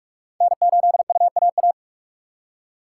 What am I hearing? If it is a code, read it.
Morse code, heard as a keyed tone, about 61 wpm, spaced as for 53 wpm, 700 Hz. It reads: N8UAA
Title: Speed Racing - USA 1x3 Call Signs 45wpm